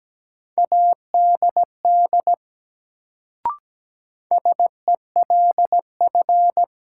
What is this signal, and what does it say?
Morse code, heard as a keyed tone, about 17 wpm, 700 Hz. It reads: ADD SELF
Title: Sentences from Top 1000 Words - Review Part 1 - New Word Drill - 17wpm